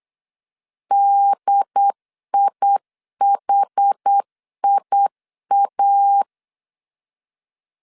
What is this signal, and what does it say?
Morse code, heard as a keyed tone, about 8 wpm, 800 Hz. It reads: DIHIA